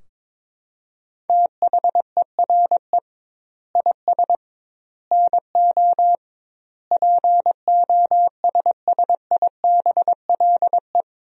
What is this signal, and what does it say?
Morse code, heard as a keyed tone, about 22 wpm, 700 Hz. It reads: THERE IS NO POSSIBLE
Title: 2x Word-Spacing - Sentences from Top 500 Words - 22wpm